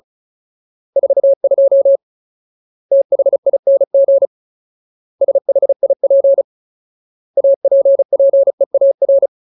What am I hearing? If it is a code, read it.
Morse code, heard as a keyed tone, about 34 wpm, 550 Hz. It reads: 42 THING SHIP APPEAR
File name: Ultimate Word List / 35WPM Ultimate